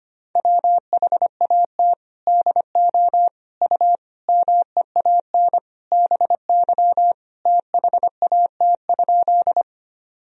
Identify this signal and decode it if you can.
Morse code, heard as a keyed tone, about 25 wpm, 700 Hz. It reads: WHAT DO U MEAN BY THAT?
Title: Sentences from Top 200 Words and Abbreviations 25wpm